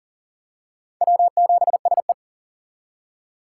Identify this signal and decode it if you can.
Morse code, heard as a keyed tone, about 38 wpm, 700 Hz. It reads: W7SE
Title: K1USN SST QSO Practice - 40wpm